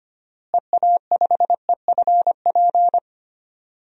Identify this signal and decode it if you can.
Morse code, heard as a keyed tone, about 25 wpm, 700 Hz. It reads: EA5EFP
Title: DXCC-100 Call Signs - 6 Characters - Repeated 2x - 25wpm